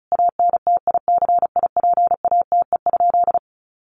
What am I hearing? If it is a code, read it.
Morse code, heard as a keyed tone, about 34 wpm, 700 Hz. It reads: ANTICIPATE?
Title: Sentences from Top 1500 Words 35wpm